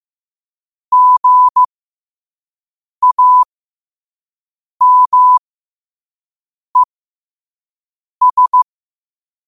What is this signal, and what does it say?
Morse code, heard as a keyed tone, about 15 wpm, spaced as for 5 wpm, 1000 Hz. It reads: GAMES
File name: qotdfiles / 5wpm